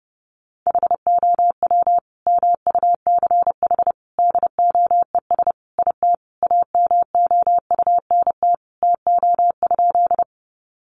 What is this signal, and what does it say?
Morse code, heard as a keyed tone, about 30 wpm, 700 Hz. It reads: HOW MUCH DOES IT AMOUNT TO?
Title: Sentences from Top 700 Words 30wpm